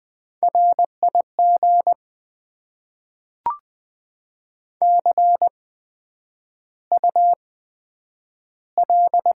Morse code, heard as a keyed tone, about 20 wpm, spaced as for 9 wpm, 700 Hz. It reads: RIG CUL
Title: Top 90 QSO Elements - 8x Character Spacing - 20wpm